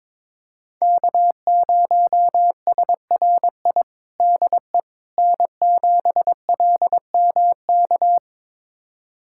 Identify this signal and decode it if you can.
Morse code, heard as a keyed tone, about 22 wpm, 700 Hz. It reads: K0SRI DE N7LMK
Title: USA Callsign DE Callsign - 5 Characters - 22wpm